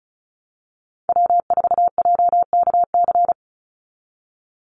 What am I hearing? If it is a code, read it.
Morse code, heard as a keyed tone, about 34 wpm, 700 Hz. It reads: W4JKC
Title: NA SOTA Chaser Callsigns 35wpm